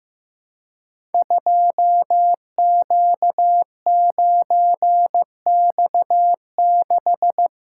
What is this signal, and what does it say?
Morse code, heard as a keyed tone, about 15 wpm, 700 Hz. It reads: 2Q9X6
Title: Code Groups - 5 Letter:Numbers - 15wpm